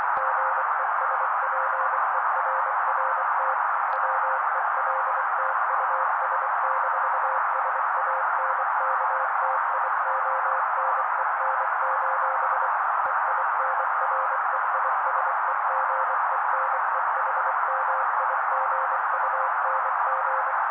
Morse code, heard as a keyed tone, about 24 wpm, 530 Hz. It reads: GESPERRT WELT US-SANKTIONEN: EINREISEGENEHMIGUNG
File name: daily / 23wpm noise